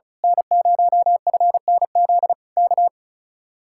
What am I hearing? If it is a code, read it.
Morse code, heard as a keyed tone, about 33 wpm, 700 Hz. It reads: N0FNZ K
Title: Speed-Racing Common QSO Phrases Encore - 35wpm